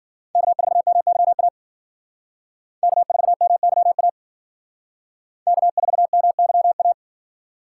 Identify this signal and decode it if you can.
Morse code, heard as a keyed tone, about 52 wpm, 700 Hz. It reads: K4MYA K4MYA K4MYA